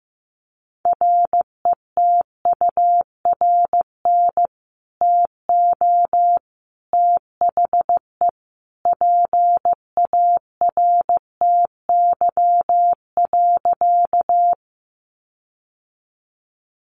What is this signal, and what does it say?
Morse code, heard as a keyed tone, about 15 wpm, 700 Hz. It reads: RETURN TO THE PARTY.
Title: Sentences from Top 1400 Words 15wpm